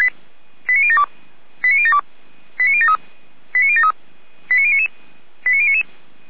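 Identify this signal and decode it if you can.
Five-tone selective call (ZVEI), heard as quick runs of five tones, tones of about 70 ms. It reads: 89072 89072 89073 89073 89099 89099
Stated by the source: Five-tone ZVEI selective call 89170, received by SDR.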